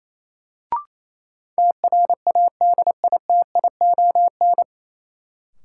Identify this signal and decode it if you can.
Morse code, heard as a keyed tone, about 28 wpm, 700 Hz. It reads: TRADITION
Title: Sentences from Top 1500 Words - New Word Drill Alt - 28wpm